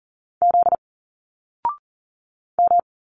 Morse code, heard as a keyed tone, about 40 wpm, 700 Hz. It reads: Z M
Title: Single Letters 40wpm